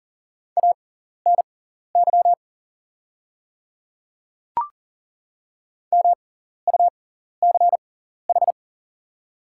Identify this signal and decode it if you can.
Morse code, heard as a keyed tone, about 40 wpm, spaced as for 14 wpm, 700 Hz. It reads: ANY MUCH
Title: 6x Character Spacing - Top 300 Words - 40wpm